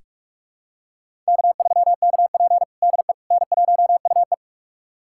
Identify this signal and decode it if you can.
Morse code, heard as a keyed tone, about 46 wpm, 700 Hz. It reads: K3KP DE N1UE
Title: USA Callsign DE Callsign - 4 Characters - 45wpm